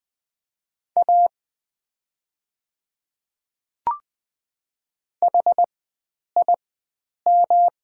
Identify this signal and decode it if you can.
Morse code, heard as a keyed tone, about 20 wpm, spaced as for 9 wpm, 700 Hz. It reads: A HIM
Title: 4x Character Spacing - Top 100 Words - 20wpm